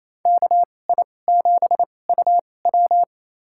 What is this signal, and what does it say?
Morse code, heard as a keyed tone, about 28 wpm, spaced as for 20 wpm, 700 Hz. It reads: KI7UW
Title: USA Call Signs - 5 Characters - 2x Character Spacing - 28wpm